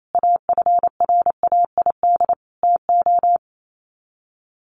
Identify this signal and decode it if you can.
Morse code, heard as a keyed tone, about 28 wpm, 700 Hz. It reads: AFRAID TO
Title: Sets of 5 Words Encore2 28wpm